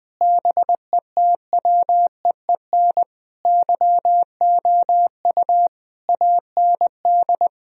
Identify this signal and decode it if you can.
Morse code, heard as a keyed tone, about 20 wpm, 700 Hz. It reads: BETWEEN YOU AND
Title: Sentences from Top 1000 Words - Review Part 2 - 20wpm